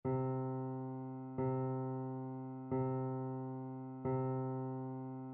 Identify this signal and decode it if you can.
There is no code.